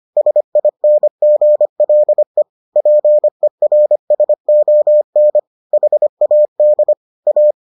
Morse code, heard as a keyed tone, about 25 wpm, 600 Hz. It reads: SINGLE PERSON HAD A